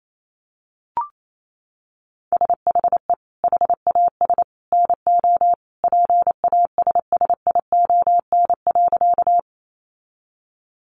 Morse code, heard as a keyed tone, about 28 wpm, 700 Hz. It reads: SHE HAS NO PASSION.
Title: Sentences from Top 1300 Words 28wpm